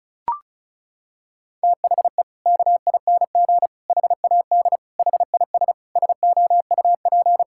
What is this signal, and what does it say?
Morse code, heard as a keyed tone, about 36 wpm, 700 Hz. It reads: THE KING HAD HIS SOUP
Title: Sentences from Top 1000 Words - Review Part 6 - 35wpm